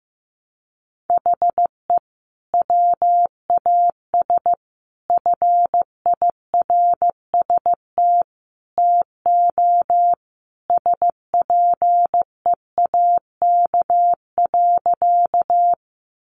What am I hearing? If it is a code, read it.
Morse code, heard as a keyed tone, about 15 wpm, 700 Hz. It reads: HE WAS FIRST TO SPEAK.
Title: Sentences from Top 1100 Words 15wpm